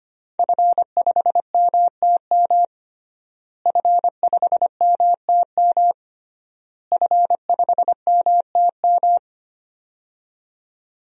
Morse code, heard as a keyed tone, about 25 wpm, 700 Hz. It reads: F5MTM F5MTM F5MTM
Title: DXCC-100 Call Signs - 5 Characters - Repeated 3x - 25wpm